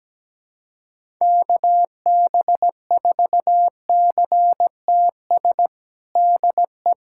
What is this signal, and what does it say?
Morse code, heard as a keyed tone, about 17 wpm, 700 Hz. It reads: KB4CTS DE
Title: USA Callsign DE Callsign 17wpm